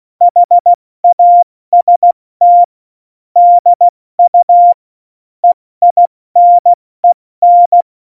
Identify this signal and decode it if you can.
Morse code, heard as a keyed tone, about 16 wpm, 700 Hz. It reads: HAST DU EINEN
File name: Morsekurs / Witze / Witz 11. September 2018, 70 F80